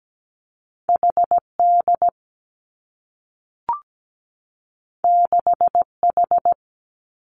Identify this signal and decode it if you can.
Morse code, heard as a keyed tone, about 17 wpm, 700 Hz. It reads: HD 6H
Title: Two Challenging Letter-Number Combinations 17wpm